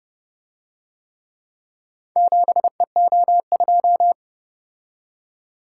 Morse code, heard as a keyed tone, about 29 wpm, 700 Hz. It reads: 7EO2